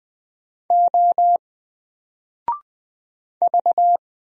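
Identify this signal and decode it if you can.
Morse code, heard as a keyed tone, about 20 wpm, 700 Hz. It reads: O V
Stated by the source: Single letter and number Morse drill at 20 wpm, from N4LCD.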